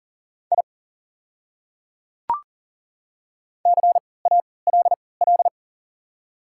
Morse code, heard as a keyed tone, about 40 wpm, spaced as for 23 wpm, 690 Hz. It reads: I CALL
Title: 3x Character Spacing - Top 200 Words - 40wpm